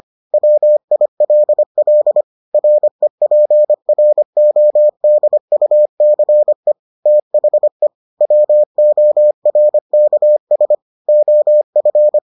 Morse code, heard as a keyed tone, about 25 wpm, 600 Hz. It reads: WILL REPRODUCE THE WORKS OF